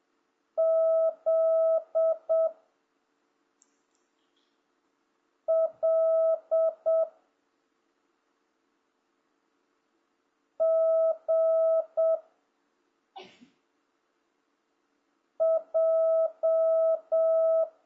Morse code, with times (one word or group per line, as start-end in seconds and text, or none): 0.6-2.5 Z
5.5-7.0 L
10.6-12.2 G
15.4-17.6 J